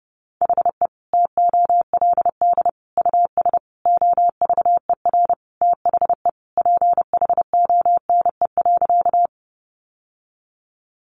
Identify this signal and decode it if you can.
Morse code, heard as a keyed tone, about 30 wpm, 700 Hz. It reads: HE TOLD US OVER THE PHONE.
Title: Sentences from Top 1100 Words 30wpm